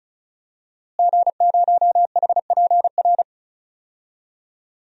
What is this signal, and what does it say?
Morse code, heard as a keyed tone, about 36 wpm, 700 Hz. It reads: G0HPR
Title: DXCC-100 Call Signs - 5 Characters - Repeated 3x - 35wpm